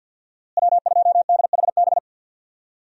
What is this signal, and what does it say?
Morse code, heard as a keyed tone, about 52 wpm, 700 Hz. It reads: W2DHB
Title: USA Call Signs - 5 Characters - Repeated 3x - 50wpm